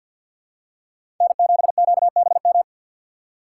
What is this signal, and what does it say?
Morse code, heard as a keyed tone, about 51 wpm, 700 Hz. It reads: N7QBM
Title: USA Call Signs - 5 Characters - 3x Character Spacing - 50wpm